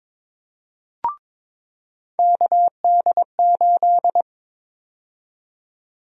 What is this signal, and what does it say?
Morse code, heard as a keyed tone, about 22 wpm, 700 Hz. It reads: KD8
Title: USA Call Sign Prefixes - 22wpm